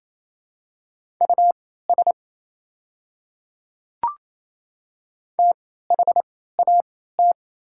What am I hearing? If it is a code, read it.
Morse code, heard as a keyed tone, about 29 wpm, spaced as for 16 wpm, 690 Hz. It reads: US THAT
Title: 3x Character Spacing - Top 200 Words - 28wpm